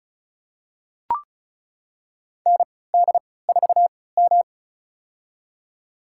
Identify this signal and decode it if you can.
Morse code, heard as a keyed tone, about 34 wpm, spaced as for 20 wpm, 700 Hz. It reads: ND4M
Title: USA Call Signs - 4 Characters - 3x Character Spacing - 35wpm